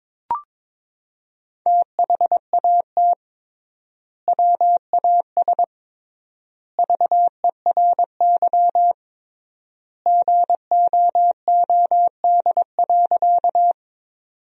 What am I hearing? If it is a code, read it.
Morse code, heard as a keyed tone, about 22 wpm, 700 Hz. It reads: THAT WAS VERY GOOD.
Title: 3x Word-Spacing - Sentences from Top 200 Words - 22wpm